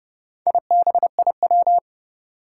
Morse code, heard as a keyed tone, about 30 wpm, 700 Hz. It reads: IBIW
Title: Four Challenging Letter-Number Combinations 30wpm